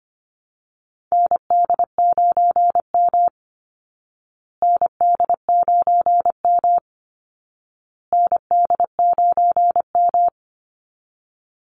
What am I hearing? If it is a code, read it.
Morse code, heard as a keyed tone, about 25 wpm, 700 Hz. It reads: ND9M ND9M ND9M
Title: Call Signs Repeated 2x - 25wpm